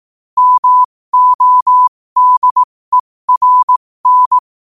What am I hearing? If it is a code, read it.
Morse code, heard as a keyed tone, about 18 wpm, spaced as for 15 wpm, 1000 Hz. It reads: MODERN
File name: qotdfiles / 15wpm